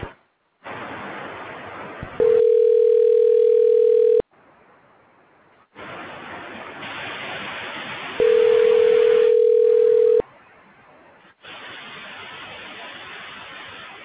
Ringback tone (North America), 440 + 480 Hz.